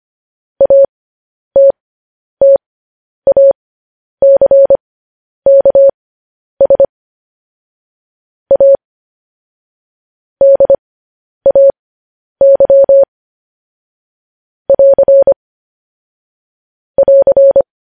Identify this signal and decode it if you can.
Morse code, heard as a keyed tone, about 25 wpm, spaced as for 10 wpm, 550 Hz. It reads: ATTACKS A DAY <AR> <AR>
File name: podcast / News headlines-1x-25x10